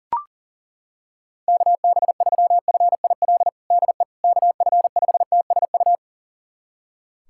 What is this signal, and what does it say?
Morse code, heard as a keyed tone, about 40 wpm, 700 Hz. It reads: KB3FIL DE KF5TSU
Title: USA Callsign DE Callsign - 6 Characters - 40wpm